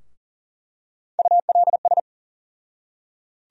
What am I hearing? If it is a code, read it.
Morse code, heard as a keyed tone, about 40 wpm, 700 Hz. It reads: ULS
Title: USA Call Sign Postfixes - 3 Letters 40wpm